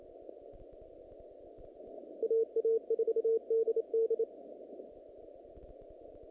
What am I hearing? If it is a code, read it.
Morse code, heard as a keyed tone, about 29 wpm, 450 Hz. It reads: AA4DD
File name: contests / naqp-cw-k5tr-op-k5pi- -2100